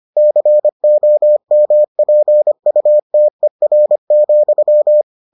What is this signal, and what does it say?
Morse code, heard as a keyed tone, about 25 wpm, 600 Hz. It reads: COMPUTER,